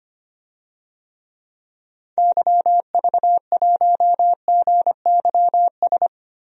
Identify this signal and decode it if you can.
Morse code, heard as a keyed tone, about 25 wpm, 700 Hz. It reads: YV1GYS
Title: DXCC-100 Call Signs - 6 Characters - 25wpm